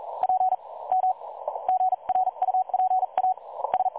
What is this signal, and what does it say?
Morse code, heard as a keyed tone, about 38 wpm, 760 Hz. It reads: P M GRUPA S
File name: dlazaawansowanych / nagraniazpasma / websdr recording 3565.0kHz